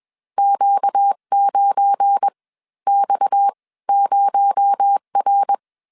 Morse code, heard as a keyed tone, about 21 wpm, spaced as for 15 wpm, 800 Hz. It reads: Q9X0R
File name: CW / DET / 1 9 1500C S C 20wpm 2310